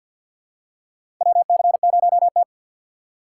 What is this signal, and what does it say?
Morse code, heard as a keyed tone, about 50 wpm, 690 Hz. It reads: WC0T
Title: USA Call Signs - 4 Characters - 6x Character Spacing - 50wpm